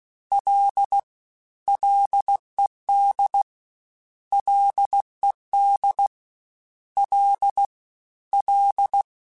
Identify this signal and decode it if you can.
Morse code, heard as a keyed tone, about 16 wpm, 800 Hz. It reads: L LED LED L L
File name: Ls / Single L m3s / Single L @15